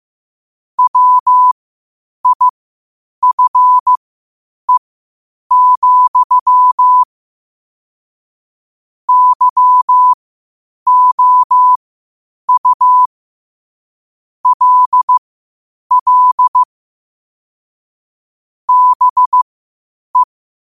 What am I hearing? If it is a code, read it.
Morse code, heard as a keyed tone, about 15 wpm, spaced as for 8 wpm, 1000 Hz. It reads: WIFE, YOU LL BE